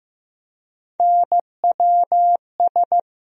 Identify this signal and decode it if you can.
Morse code, heard as a keyed tone, about 15 wpm, 700 Hz. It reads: NWS
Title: USA Call Sign Postfixes - 3 Letters 15wpm